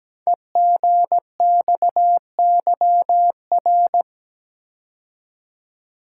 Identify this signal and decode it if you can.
Morse code, heard as a keyed tone, about 17 wpm, 700 Hz. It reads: EGXYR
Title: Code Groups - 5 Letters - 17wpm